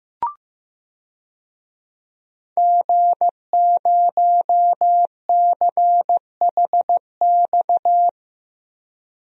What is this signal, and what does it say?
Morse code, heard as a keyed tone, about 15 wpm, 700 Hz. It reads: G0CHX
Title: DXCC-100 Call Signs - 5 Characters - Repeated 2x - 15wpm